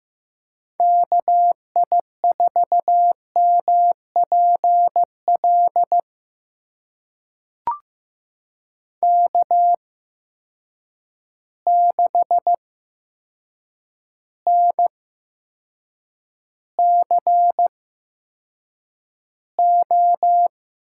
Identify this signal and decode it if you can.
Morse code, heard as a keyed tone, about 15 wpm, spaced as for 7 wpm, 700 Hz. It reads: KI4MPL K6NCO
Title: USA Call Signs - 8x Character Spacing - 15wpm